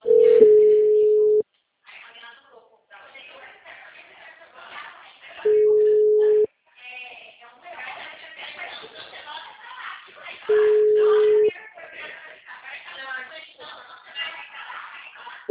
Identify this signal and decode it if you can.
Ringback tone (Europe, 425 Hz), with speech.